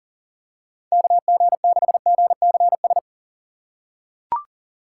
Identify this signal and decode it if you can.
Morse code, heard as a keyed tone, about 40 wpm, 700 Hz. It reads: KG6GCS
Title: USA Call Signs - 6 Characters - 3x Character Spacing - 40wpm